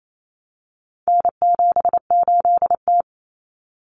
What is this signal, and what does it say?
Morse code, heard as a keyed tone, about 28 wpm, 700 Hz. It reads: N78T
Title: Common Cut Numbers - 4-Digit 28wpm